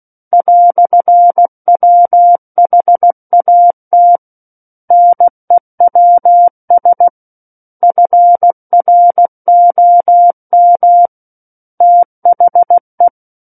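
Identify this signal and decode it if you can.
Morse code, heard as a keyed tone, about 16 wpm, 700 Hz. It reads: "WHAT NEWS FROM THE
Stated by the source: Morse practice of book text at 16 wpm.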